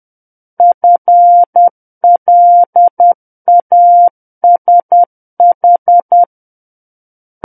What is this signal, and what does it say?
Morse code, heard as a keyed tone, about 10 wpm, 700 Hz. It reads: FLASH
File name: Book One / 01-Chapter 1 - 10 WPM